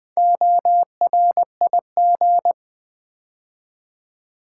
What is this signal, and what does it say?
Morse code, heard as a keyed tone, about 20 wpm, 700 Hz. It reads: ORIG
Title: Build A Word - 4 Letters At A Time - 10-Letter Words - 20wpm